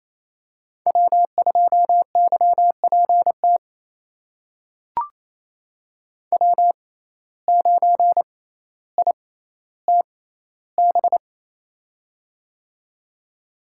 Morse code, heard as a keyed tone, about 28 wpm, spaced as for 14 wpm, 700 Hz. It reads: W2YPT W9ITB
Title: USA Call Signs - 6x Character Spacing - 28wpm